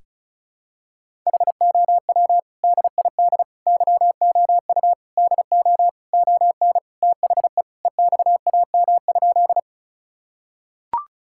Morse code, heard as a keyed tone, about 36 wpm, 700 Hz. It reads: HOW DID YOU DO ON THE EXAM?
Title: Sentences from Top 1000 Words - Review Part 5 - 35wpm